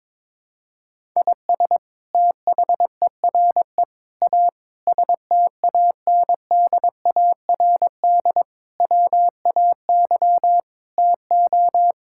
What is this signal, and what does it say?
Morse code, heard as a keyed tone, about 22 wpm, 700 Hz. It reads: IS THERE A STANDARD WAY TO
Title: Sentences from Top 1000 Words - Review Part 6 - 22wpm